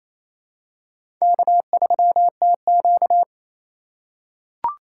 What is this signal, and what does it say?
Morse code, heard as a keyed tone, about 27 wpm, 700 Hz. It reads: K3TQ
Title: USA Call Signs - 4 Characters - 2x Character Spacing - 28wpm